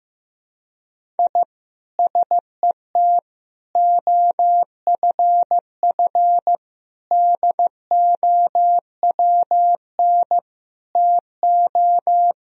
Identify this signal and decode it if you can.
Morse code, heard as a keyed tone, about 16 wpm, 700 Hz. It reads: I SET OFF DOWN TO
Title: Sentences from Top 1000 Words - Review Part 1 - 15wpm